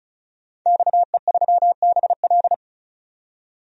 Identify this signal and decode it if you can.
Morse code, heard as a keyed tone, about 36 wpm, 700 Hz. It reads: XE3BL
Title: DXCC-100 Call Signs - 5 Characters - 35wpm